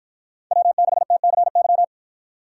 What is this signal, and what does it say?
Morse code, heard as a keyed tone, about 53 wpm, 700 Hz. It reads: W6TCY